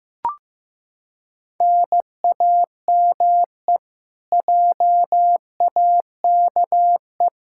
Morse code, heard as a keyed tone, about 15 wpm, 700 Hz. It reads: NAME JAKE
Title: Practice QSO - Exchange 1 - Beginning - 15wpm